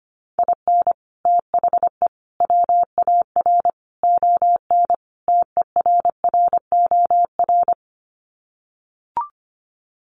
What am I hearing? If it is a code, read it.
Morse code, heard as a keyed tone, about 25 wpm, 700 Hz. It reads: IN THE WAR ON TERROR
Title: Sets of 5 Words Encore2 25wpm